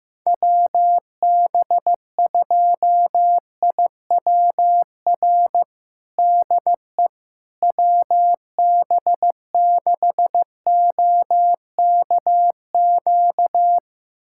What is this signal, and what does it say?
Morse code, heard as a keyed tone, about 15 wpm, 700 Hz. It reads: WB2IWR DE WB6OKQ
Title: USA Callsign DE Callsign 15wpm